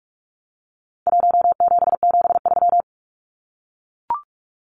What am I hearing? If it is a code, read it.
Morse code, heard as a keyed tone, about 44 wpm, 700 Hz. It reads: 1773
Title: Numbers - Contest Style - 4-Digit 45wpm